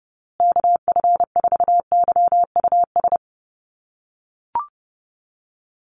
Morse code, heard as a keyed tone, about 30 wpm, 700 Hz. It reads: KF4YUS
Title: USA Call Signs Repeated 3x 30wpm